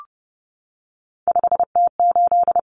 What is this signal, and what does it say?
Morse code, heard as a keyed tone, about 28 wpm, 700 Hz. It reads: 5T8
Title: Common Cut Numbers - 3-Digit 30wpm